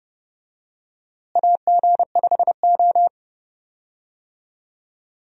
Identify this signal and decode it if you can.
Morse code, heard as a keyed tone, about 29 wpm, 700 Hz. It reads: AG5O
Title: USA Call Signs - 4 Characters - Repeated 2x - 30wpm